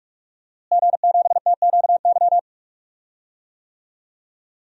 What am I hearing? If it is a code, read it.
Morse code, heard as a keyed tone, about 46 wpm, 700 Hz. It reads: G7TQY
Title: DXCC-100 Call Signs - 5 Characters - Repeated 3x - 45wpm